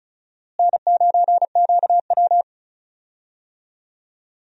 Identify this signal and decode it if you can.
Morse code, heard as a keyed tone, about 36 wpm, 700 Hz. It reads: N9QW
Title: USA Call Signs - 4 Characters - Repeated 2x - 35wpm